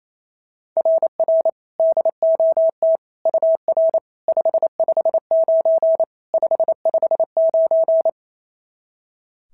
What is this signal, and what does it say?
Morse code, heard as a keyed tone, about 28 wpm, 690 Hz. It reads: RR DOT UR 559 559
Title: SOTA QSO Practice 28wpm